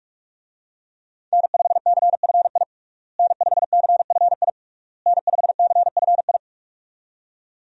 Morse code, heard as a keyed tone, about 46 wpm, 700 Hz. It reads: N5CFI N5CFI N5CFI
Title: USA Call Signs - 5 Characters - Repeated 3x - 45wpm